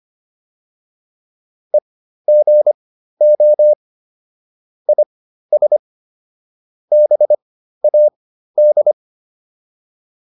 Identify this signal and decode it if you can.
Morse code, heard as a keyed tone, about 25 wpm, spaced as for 13 wpm, 600 Hz. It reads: EGO IS BAD